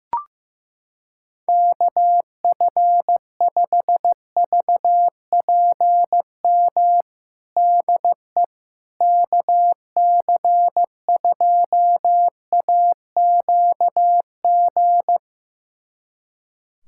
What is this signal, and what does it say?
Morse code, heard as a keyed tone, about 16 wpm, 700 Hz. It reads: KF5VPM DE KC2AQG
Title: USA Callsign DE Callsign - 6 Characters - 15wpm